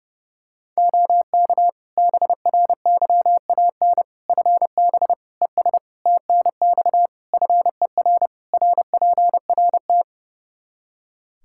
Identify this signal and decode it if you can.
Morse code, heard as a keyed tone, about 30 wpm, 700 Hz. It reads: OK BRYAN FB ES TNX FER RPRT
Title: Practice QSO - Exchange 2 - Info - 30wpm